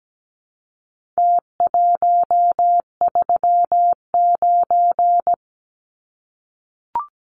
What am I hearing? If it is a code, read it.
Morse code, heard as a keyed tone, about 17 wpm, 700 Hz. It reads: T139
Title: Numbers - Contest Style - 4-Digit 17wpm